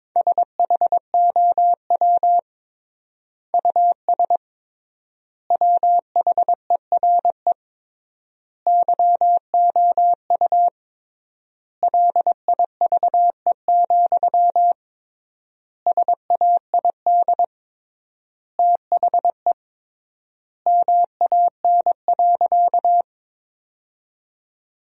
Morse code, heard as a keyed tone, about 22 wpm, 700 Hz. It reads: SHOW US WHERE YOU LIVE, SAID THE MAN.
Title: 3x Word-Spacing - Sentences from Top 200 Words - 22wpm